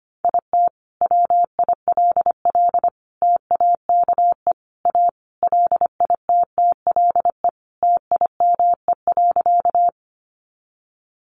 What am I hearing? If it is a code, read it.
Morse code, heard as a keyed tone, about 25 wpm, 700 Hz. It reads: IT WILL TAKE A LITTLE TIME.
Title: Sentences from Top 200 Words 25wpm